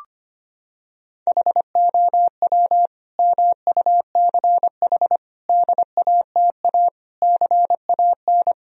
Morse code, heard as a keyed tone, about 25 wpm, 700 Hz. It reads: HOW MUCH DATA CAN
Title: Sentences from Top 1000 Words - Review Part 4 - 25wpm